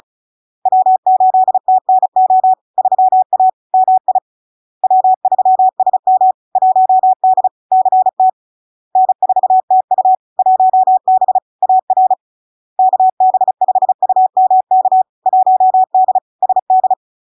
Morse code, heard as a keyed tone, about 36 wpm, 750 Hz. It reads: W8TNO 3A MI W3SM 1D CT N4TU 1B AR KB5UMK 1D SD
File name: LARK-FD-2022-Log-in-CW-35wpm-750-Hz